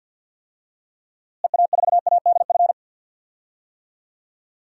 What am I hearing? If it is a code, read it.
Morse code, heard as a keyed tone, about 53 wpm, 700 Hz. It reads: EA4ADF
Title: DXCC-100 Call Signs - 6 Characters - Repeated 2x - 50wpm